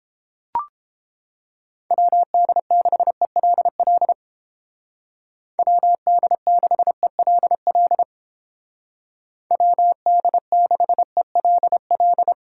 Morse code, heard as a keyed tone, about 30 wpm, 700 Hz. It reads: WD6ELL WD6ELL WD6ELL